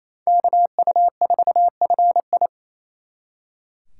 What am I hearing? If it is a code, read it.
Morse code, heard as a keyed tone, about 33 wpm, 700 Hz. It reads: KU4FI
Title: Speed Racing - USA 2x2 Call Signs 28wpm